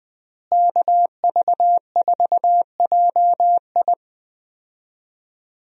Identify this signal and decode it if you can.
Morse code, heard as a keyed tone, about 21 wpm, 700 Hz. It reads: KV4JI
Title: USA Call Signs - 5 Characters - Repeated 3x - 20wpm